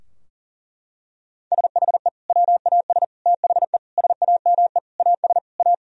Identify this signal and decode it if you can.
Morse code, heard as a keyed tone, about 40 wpm, 700 Hz. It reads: SHE WAS THE SAME AS A
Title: Sentences from Top 1000 Words - Review Part 2 - 40wpm